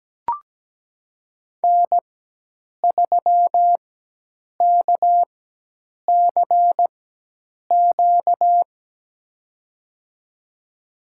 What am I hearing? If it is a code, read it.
Morse code, heard as a keyed tone, about 17 wpm, spaced as for 8 wpm, 700 Hz. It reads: N3KCQ
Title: USA Call Signs - 5 Characters - 4x Character Spacing - 17wpm